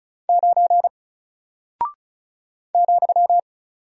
Morse code, heard as a keyed tone, about 34 wpm, 700 Hz. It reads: 9 ,
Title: Single Character (Letter:Number:Punctuation:Prosign) 35wpm